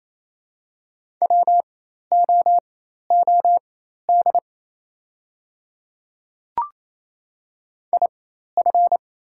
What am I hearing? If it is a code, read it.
Morse code, heard as a keyed tone, about 28 wpm, spaced as for 13 wpm, 700 Hz. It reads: WOOD IF
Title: 4x Character Spacing - Top 300 Words - 28wpm